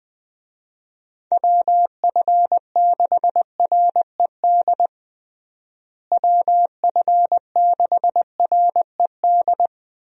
Morse code, heard as a keyed tone, about 20 wpm, 700 Hz. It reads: WF6RED WF6RED
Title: USA Call Signs - 6 Characters - Repeated 3x - 20wpm